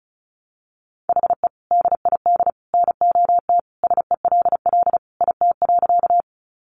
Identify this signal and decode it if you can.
Morse code, heard as a keyed tone, about 34 wpm, 700 Hz. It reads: HE DID NOT SELL IT.
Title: Sentences from Top 1100 Words 35wpm